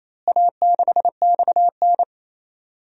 Morse code, heard as a keyed tone, about 27 wpm, 700 Hz. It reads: A6XN
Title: DXCC-100 Call Signs - 4 Characters - Repeated 3x - 28wpm